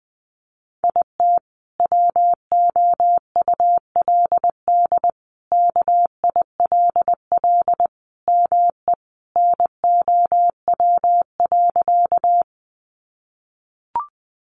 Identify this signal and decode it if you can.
Morse code, heard as a keyed tone, about 20 wpm, 700 Hz. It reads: IT WOULD KILL ME NOW.
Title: Sentences from Top 1400 Words 20wpm